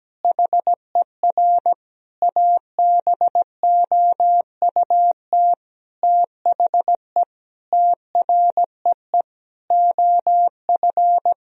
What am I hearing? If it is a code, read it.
Morse code, heard as a keyed tone, about 17 wpm, 700 Hz. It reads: HER ABOUT THE TREE OF
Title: Sentences from Top 1000 Words - Review Part 1 - 17wpm